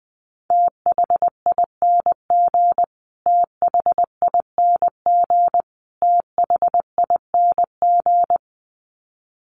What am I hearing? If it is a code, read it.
Morse code, heard as a keyed tone, about 20 wpm, 700 Hz. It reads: THING THING THING
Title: Top 300 Words Repeated 20wpm